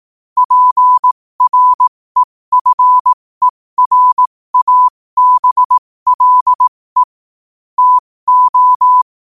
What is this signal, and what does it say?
Morse code, heard as a keyed tone, about 18 wpm, spaced as for 15 wpm, 1000 Hz. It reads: PREFERABLE TO
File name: qotdfiles / 15wpm